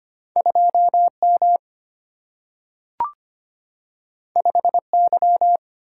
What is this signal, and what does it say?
Morse code, heard as a keyed tone, about 25 wpm, 700 Hz. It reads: 2M 5Y
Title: Single-Letter Single-Number 25wpm